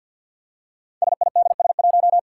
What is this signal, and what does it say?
Morse code, heard as a keyed tone, about 53 wpm, 700 Hz. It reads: SIDS1
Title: Five Challenging Letter-Number Combinations 50wpm